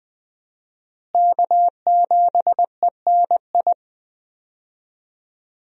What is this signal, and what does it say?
Morse code, heard as a keyed tone, about 21 wpm, 700 Hz. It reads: K7ENI
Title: USA Call Signs - 5 Characters - Repeated 2x - 20wpm